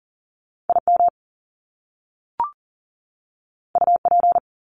Morse code, heard as a keyed tone, about 39 wpm, 700 Hz. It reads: IM UP
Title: Top Two Letter Combinations in Words 40wpm